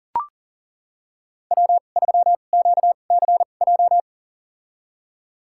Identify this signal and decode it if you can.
Morse code, heard as a keyed tone, about 39 wpm, spaced as for 29 wpm, 700 Hz. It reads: W3QCJ
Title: USA Call Signs - 2x Character Spacing - 40wpm